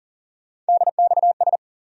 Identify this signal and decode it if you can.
Morse code, heard as a keyed tone, about 40 wpm, 700 Hz. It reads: DXS
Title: USA Call Sign Postfixes - 3 Letters 40wpm